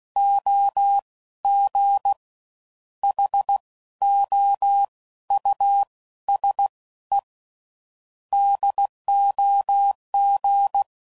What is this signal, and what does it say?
Morse code, heard as a keyed tone, about 16 wpm, 800 Hz. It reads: O G H O U S E DOG
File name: Suffixes / General / Houses